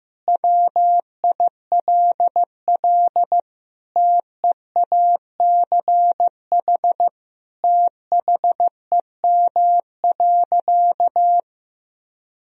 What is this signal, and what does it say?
Morse code, heard as a keyed tone, about 15 wpm, 700 Hz. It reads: WILL TEACH THEM.
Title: Sentences from Top 1000 Words - Review Part 2 - 15wpm